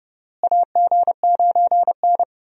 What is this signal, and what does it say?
Morse code, heard as a keyed tone, about 32 wpm, 700 Hz. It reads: AG9N
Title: USA Call Signs - 4 Characters - Repeated 3x - 30wpm